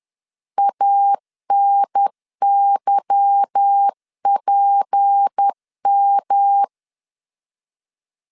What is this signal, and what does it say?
Morse code, heard as a keyed tone, about 10 wpm, 800 Hz. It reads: ANYPM
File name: LEZIONI / 8 YJ45 / 8 Y 10wpm 2615